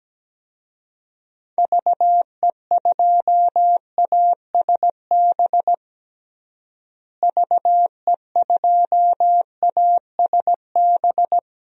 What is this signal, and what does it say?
Morse code, heard as a keyed tone, about 17 wpm, 700 Hz. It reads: VE2ASB VE2ASB
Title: DXCC-100 Call Signs - 6 Characters - Repeated 3x - 17wpm